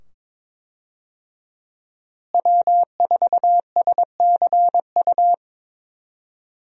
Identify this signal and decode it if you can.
Morse code, heard as a keyed tone, about 21 wpm, 700 Hz. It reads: W4SCU
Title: USA Call Signs - 5 Characters - Repeated 2x - 22wpm